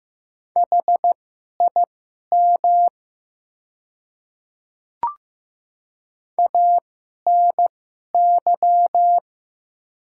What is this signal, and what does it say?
Morse code, heard as a keyed tone, about 15 wpm, spaced as for 11 wpm, 690 Hz. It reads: HIM ANY
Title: 2x Character Spacing - Top 200 Words - 15wpm